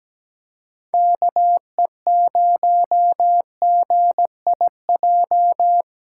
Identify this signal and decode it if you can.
Morse code, heard as a keyed tone, about 17 wpm, 700 Hz. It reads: KE0GIJ
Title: USA Callsign DE Callsign - 6 Characters - 17wpm